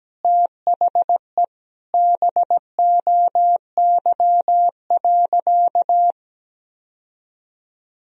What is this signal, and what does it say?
Morse code, heard as a keyed tone, about 17 wpm, 700 Hz. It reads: THE BOY.
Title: Sentences from Top 1000 Words - Review Part 4 - 17wpm